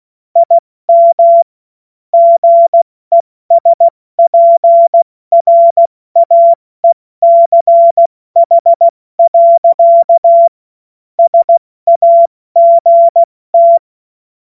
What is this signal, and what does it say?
Morse code, heard as a keyed tone, about 16 wpm, 680 Hz. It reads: IM GESPRAECH. SAGT
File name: Morsekurs / Witze / Witz 08. Februar 2022 70 F80